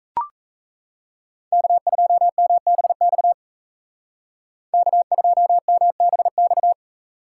Morse code, heard as a keyed tone, about 34 wpm, 700 Hz. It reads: K2MBX K2MBX
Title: Speed Racing - USA 1x3 Call Signs 28wpm